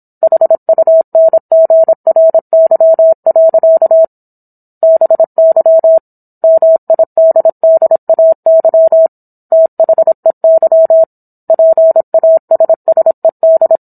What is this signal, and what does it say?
Morse code, heard as a keyed tone, about 27 wpm, 650 Hz. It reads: HUNGRY. BY MIDDAY THEY PASSED